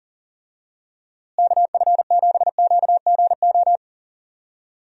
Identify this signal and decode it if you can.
Morse code, heard as a keyed tone, about 40 wpm, 700 Hz. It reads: KF7QGO
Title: USA Call Signs - 6 Characters - Repeated 2x - 40wpm